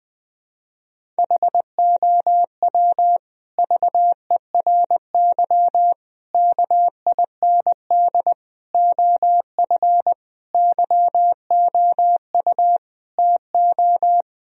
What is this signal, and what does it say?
Morse code, heard as a keyed tone, about 20 wpm, 700 Hz. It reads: HOW VERY KIND OF YOU TO
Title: Sentences from Top 1000 Words - Review Part 1 - 20wpm